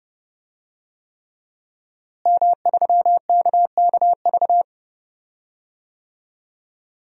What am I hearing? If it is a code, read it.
Morse code, heard as a keyed tone, about 30 wpm, 700 Hz. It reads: M3KKV
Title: DXCC-100 Call Signs - 5 Characters - Repeated 3x - 30wpm